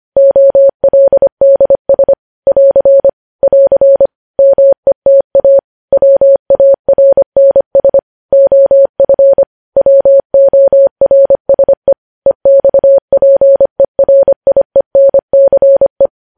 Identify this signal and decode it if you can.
Morse code, heard as a keyed tone, about 25 wpm, 550 Hz. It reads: OLDS <AR> <AR> META WARNS OF WORSE EXPERIENCE
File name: podcast / News headlines-1x-25x25